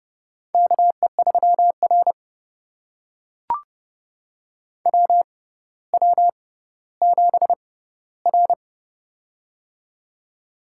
Morse code, heard as a keyed tone, about 30 wpm, spaced as for 15 wpm, 700 Hz. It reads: KE3R WW7R
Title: USA Call Signs - 4 Characters - 6x Character Spacing - 30wpm